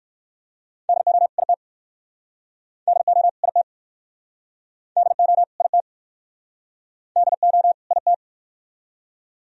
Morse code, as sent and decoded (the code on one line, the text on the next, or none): -.. --- / .. - / -.. --- / .. - / -.. --- / .. - / -.. --- / .. -
DO IT DO IT DO IT DO IT